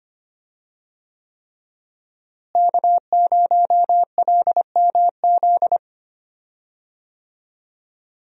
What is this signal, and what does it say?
Morse code, heard as a keyed tone, about 25 wpm, 700 Hz. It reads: K0LMZ
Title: USA Call Signs - 5 Characters - Repeated 3x - 25wpm